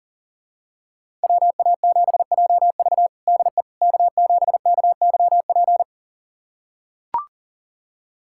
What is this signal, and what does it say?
Morse code, heard as a keyed tone, about 40 wpm, 700 Hz. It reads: WA7JV DE K7KYP
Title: USA Callsign DE Callsign - 5 Characters - 40wpm